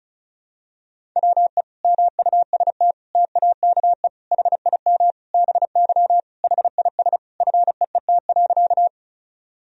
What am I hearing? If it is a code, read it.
Morse code, heard as a keyed tone, about 34 wpm, 700 Hz. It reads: WE MUST TAKE HIM BY HIS FEET.
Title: Sentences from Top 1000 Words - Review Part 1 - 35wpm